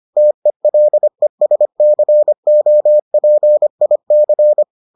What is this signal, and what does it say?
Morse code, heard as a keyed tone, about 25 wpm, 600 Hz. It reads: TELESCOPIC